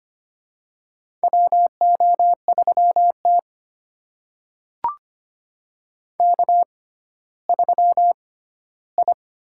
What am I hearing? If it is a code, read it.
Morse code, heard as a keyed tone, about 25 wpm, spaced as for 14 wpm, 700 Hz. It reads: WO3T K3I
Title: USA Call Signs - 4 Characters - 6x Character Spacing - 25wpm